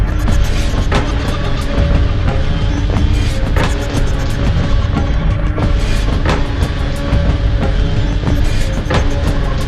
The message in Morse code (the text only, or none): none